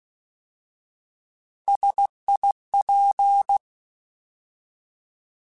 Morse code, heard as a keyed tone, about 16 wpm, 800 Hz. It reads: SIP
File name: Alphabetical / Alphabetical mp3s / Ss @15